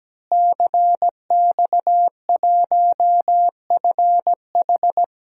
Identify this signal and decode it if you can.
Morse code, heard as a keyed tone, about 17 wpm, 700 Hz. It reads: CX1FH
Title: DXCC-100 Call Signs - 5 Characters - Repeated 2x - 17wpm